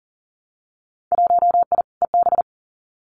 Morse code, heard as a keyed tone, about 38 wpm, 700 Hz. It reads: 1I EB
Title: Winter Field Day Exchanges Encore 40wpm